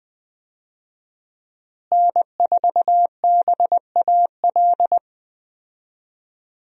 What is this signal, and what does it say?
Morse code, heard as a keyed tone, about 20 wpm, 700 Hz. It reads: N4BAL
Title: USA Call Signs - 5 Characters - Repeated 3x - 20wpm